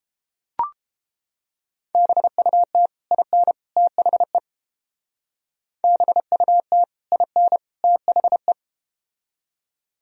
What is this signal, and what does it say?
Morse code, heard as a keyed tone, about 31 wpm, 700 Hz. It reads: BUT IN THE BUT IN THE